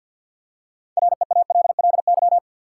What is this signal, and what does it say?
Morse code, heard as a keyed tone, about 52 wpm, 700 Hz. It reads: REALLY
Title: Six Letter Words - 50wpm